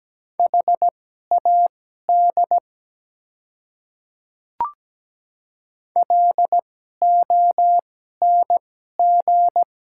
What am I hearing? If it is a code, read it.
Morse code, heard as a keyed tone, about 17 wpm, spaced as for 12 wpm, 700 Hz. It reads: HAD LONG
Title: 2x Character Spacing - Top 100 Words - 17wpm